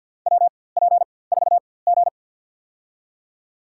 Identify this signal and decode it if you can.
Morse code, heard as a keyed tone, about 53 wpm, spaced as for 24 wpm, 700 Hz. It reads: WP4G
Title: USA Call Signs - 4x Character Spacing - 50wpm